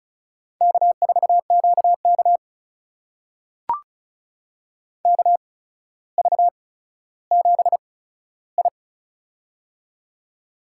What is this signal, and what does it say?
Morse code, heard as a keyed tone, about 36 wpm, spaced as for 15 wpm, 700 Hz. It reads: K4QK KV7I
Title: USA Call Signs - 4 Characters - 8x Character Spacing - 35wpm